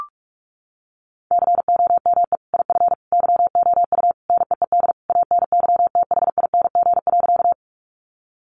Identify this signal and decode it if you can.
Morse code, heard as a keyed tone, about 44 wpm, 700 Hz. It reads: COME IF YOU NEED ANYTHING.
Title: Sentences from Top 1000 Words 45wpm